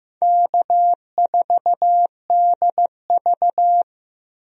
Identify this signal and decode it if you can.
Morse code, heard as a keyed tone, about 15 wpm, 700 Hz. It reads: K4DV
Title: USA Call Signs - 4 Characters - Repeated 2x - 15wpm